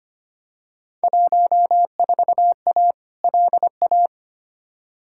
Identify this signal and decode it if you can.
Morse code, heard as a keyed tone, about 25 wpm, 700 Hz. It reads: 14A LA
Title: Field Day Exchanges Encore - 25wpm